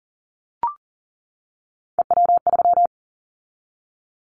Morse code, heard as a keyed tone, about 40 wpm, 700 Hz. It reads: EW3
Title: Callsign Prefixes for Top 100 DXCC - 40wpm